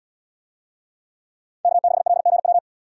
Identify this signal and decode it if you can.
Morse code, heard as a keyed tone, about 70 wpm, 700 Hz. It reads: K6LGW